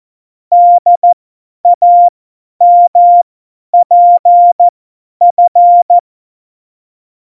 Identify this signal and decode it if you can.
Morse code, heard as a keyed tone, about 14 wpm, spaced as for 10 wpm, 700 Hz. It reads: DAMPF
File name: Morsekurs / Witze / Witz 12. November 2019, 50 F70